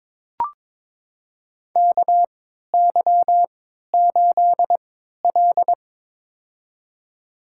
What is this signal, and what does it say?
Morse code, heard as a keyed tone, about 22 wpm, spaced as for 12 wpm, 700 Hz. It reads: KY8L
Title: USA Call Signs - 3x Character Spacing - 22wpm